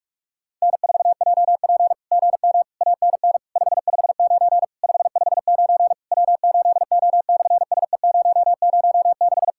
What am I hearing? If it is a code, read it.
Morse code, heard as a keyed tone, about 42 wpm, 700 Hz. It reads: N4JP GM ANN 559 559 W8O/SE006
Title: SOTA QSO Practice 45wpm